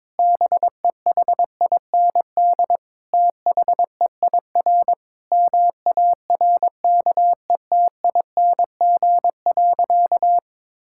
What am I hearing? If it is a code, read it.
Morse code, heard as a keyed tone, about 22 wpm, 700 Hz. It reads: BEHIND THEIR MARKETING.
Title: Sentences from Top 1000 Words - Review Part 5 - 22wpm